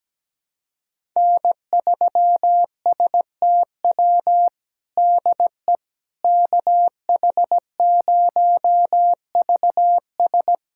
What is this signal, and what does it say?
Morse code, heard as a keyed tone, about 17 wpm, 700 Hz. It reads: N3STW DE KH0VS
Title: USA Callsign DE Callsign - 5 Characters - 17wpm